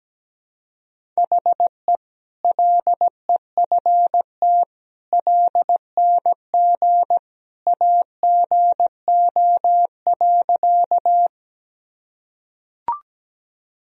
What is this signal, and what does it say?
Morse code, heard as a keyed tone, about 17 wpm, 700 Hz. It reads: HE LEFT LNG AGO.
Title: Sentences from Top 500 Words and Abbreviations 17wpm